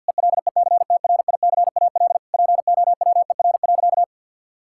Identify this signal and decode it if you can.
Morse code, heard as a keyed tone, about 52 wpm, 700 Hz. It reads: ELECTRICAL POWER.
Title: Sentences from Top 1000 Words - Review Part 6 - 50wpm